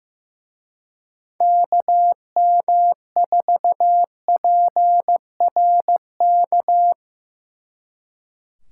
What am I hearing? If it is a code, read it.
Morse code, heard as a keyed tone, about 17 wpm, 700 Hz. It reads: KM4PRK
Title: Speed Racing - USA Call Signs 15wpm